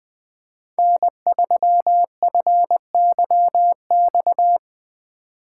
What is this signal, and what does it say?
Morse code, heard as a keyed tone, about 20 wpm, 700 Hz. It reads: N3FYX
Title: Speed Racing - USA 1x3 Call Signs 17wpm